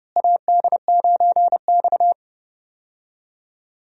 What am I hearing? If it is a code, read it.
Morse code, heard as a keyed tone, about 32 wpm, 700 Hz. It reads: AD9X
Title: USA Call Signs - 4 Characters - Repeated 2x - 30wpm